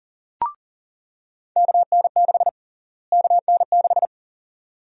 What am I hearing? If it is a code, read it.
Morse code, heard as a keyed tone, about 37 wpm, 700 Hz. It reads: KN6 KN6
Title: USA Call Sign Prefixes - Repeated 2x - 40wpm